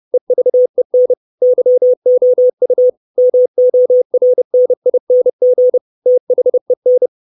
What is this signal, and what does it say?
Morse code, heard as a keyed tone, about 30 wpm, 500 Hz. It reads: EVEN YOU MORNING THEN
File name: 632 Words 30WPM